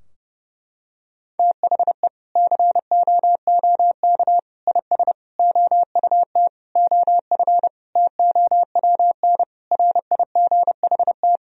Morse code, heard as a keyed tone, about 30 wpm, 700 Hz. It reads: THE COOK IS OUT OF TOWN RIGHT
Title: Sentences from Top 1000 Words - Review Part 4 - 30wpm